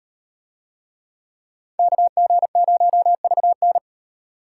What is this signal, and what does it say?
Morse code, heard as a keyed tone, about 38 wpm, 700 Hz. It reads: KG0VN